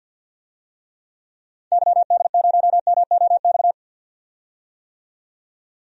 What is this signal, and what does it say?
Morse code, heard as a keyed tone, about 52 wpm, 700 Hz. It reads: YD0MOX
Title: DXCC-100 Call Signs - 6 Characters - 50wpm